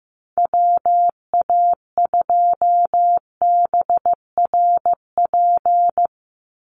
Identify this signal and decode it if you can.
Morse code, heard as a keyed tone, about 15 wpm, 700 Hz. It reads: WA2BRP
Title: USA Call Signs 15wpm